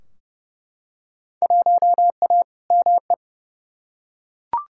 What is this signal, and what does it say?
Morse code, heard as a keyed tone, about 30 wpm, 700 Hz. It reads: 1A ME
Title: Field Day Exchanges Encore - 30wpm